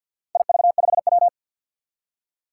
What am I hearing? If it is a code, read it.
Morse code, heard as a keyed tone, about 53 wpm, 700 Hz. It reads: IV5W
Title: Four Challenging Letter-Number Combinations 50wpm